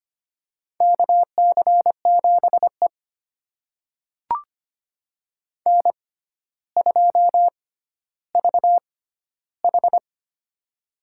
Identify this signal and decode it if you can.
Morse code, heard as a keyed tone, about 25 wpm, spaced as for 13 wpm, 700 Hz. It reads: KC7E N2VH
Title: USA Call Signs - 4 Characters - 6x Character Spacing - 25wpm